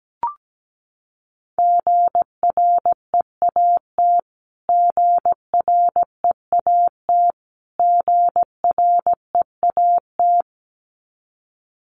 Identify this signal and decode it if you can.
Morse code, heard as a keyed tone, about 17 wpm, 700 Hz. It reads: GREAT GREAT GREAT
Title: Top 200 Words Repeated 17wpm